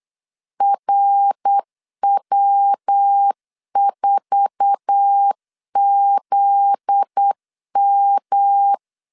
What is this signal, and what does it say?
Morse code, heard as a keyed tone, about 8 wpm, 800 Hz. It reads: RW4ZM
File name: LEZIONI / 8 YJ45 / 8 YJ4 8wpm 3246